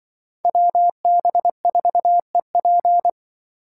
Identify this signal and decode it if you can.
Morse code, heard as a keyed tone, about 24 wpm, 700 Hz. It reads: WB4EP
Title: Speed Racing - USA 2x2 Call Signs 20wpm